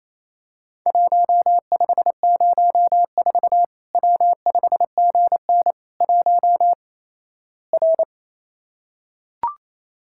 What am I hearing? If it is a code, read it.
Morse code, heard as a keyed tone, about 28 wpm, 700 Hz. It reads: 1504 W5GN 1 R
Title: Worked All Europe DX Contest - QTC Exchange - 28wpm